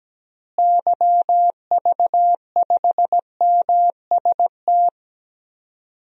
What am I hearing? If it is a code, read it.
Morse code, heard as a keyed tone, about 17 wpm, 700 Hz. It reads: YV5MST
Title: DXCC-100 Call Signs - 6 Characters - Repeated 2x - 17wpm